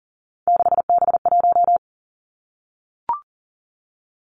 Morse code, heard as a keyed tone, about 39 wpm, 700 Hz. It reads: 6B1
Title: Three Challenging Letter-Number Combinations 40wpm